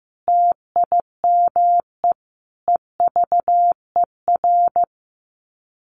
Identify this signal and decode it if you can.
Morse code, heard as a keyed tone, about 15 wpm, 700 Hz. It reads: TIME EVER
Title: Sets of 5 Words Encore2 15wpm